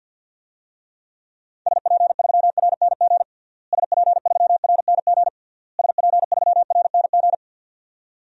Morse code, heard as a keyed tone, about 52 wpm, 700 Hz. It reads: SP3RNG SP3RNG SP3RNG
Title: DXCC-100 Call Signs - 6 Characters - Repeated 3x - 50wpm